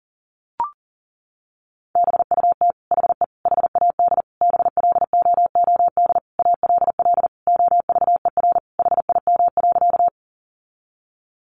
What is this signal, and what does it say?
Morse code, heard as a keyed tone, about 40 wpm, 700 Hz. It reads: BUT HE HAD BLOOD ALL OVER HIM.
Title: Sentences from Top 1100 Words 40wpm